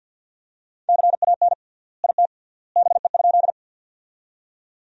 Morse code, heard as a keyed tone, about 50 wpm, 700 Hz. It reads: CAN IT BE?
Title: 3x Word-Spacing - Sentences from Top 100 Words - 50wpm